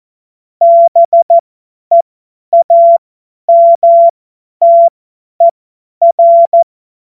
Morse code, heard as a keyed tone, about 14 wpm, spaced as for 10 wpm, 680 Hz. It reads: BEAMTER
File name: Morsekurs / Witze / Witz 02. Mai 2023 50 F70